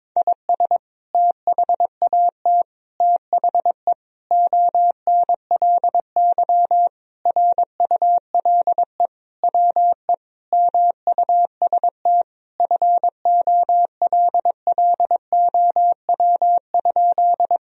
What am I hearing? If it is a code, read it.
Morse code, heard as a keyed tone, about 22 wpm, 700 Hz. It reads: IS THAT THE ONLY RULE WE MUST FOLLOW?
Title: Sentences from Top 1000 Words - Review Part 2 - 22wpm